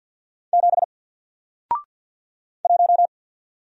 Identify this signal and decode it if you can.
Morse code, heard as a keyed tone, about 48 wpm, 700 Hz. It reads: 7 1
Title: Single Letter-Number Alt 50wpm